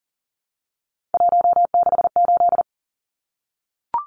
Morse code, heard as a keyed tone, about 40 wpm, 700 Hz. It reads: 168
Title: Numbers - 3-Digit 40wpm